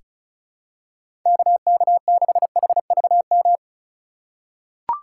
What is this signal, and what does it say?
Morse code, heard as a keyed tone, about 35 wpm, 700 Hz. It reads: KK6HVM
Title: USA Call Signs - 6 Characters - 3x Character Spacing - 35wpm